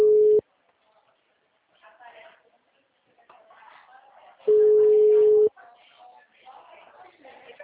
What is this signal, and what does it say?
Ringback tone (Europe, 425 Hz), with speech.